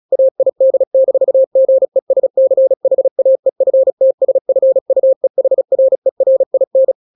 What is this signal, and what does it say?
Morse code, heard as a keyed tone, about 36 wpm, 530 Hz. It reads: AID-GESCHAEFTSFUEHRERIN
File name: daily / 35wpm